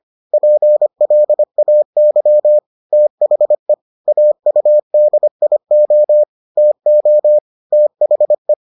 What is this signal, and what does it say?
Morse code, heard as a keyed tone, about 25 wpm, 600 Hz. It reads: PLAY THE AUDIO TO THE